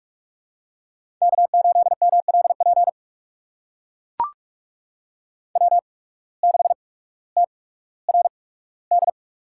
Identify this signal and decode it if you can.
Morse code, heard as a keyed tone, about 44 wpm, spaced as for 19 wpm, 700 Hz. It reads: K8MLP W6TRD
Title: USA Call Signs - 5 Characters - 8x Character Spacing - 45wpm